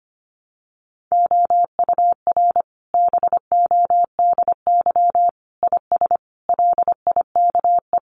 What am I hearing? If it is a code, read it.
Morse code, heard as a keyed tone, about 25 wpm, 700 Hz. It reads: OUR BODY IS LIKE
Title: Sentences from Top 1000 Words 25wpm